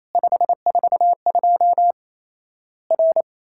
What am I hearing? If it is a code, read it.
Morse code, heard as a keyed tone, about 28 wpm, 700 Hz. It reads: 542 R
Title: Worked All Europe DX Contest - QTC Exchange - 28wpm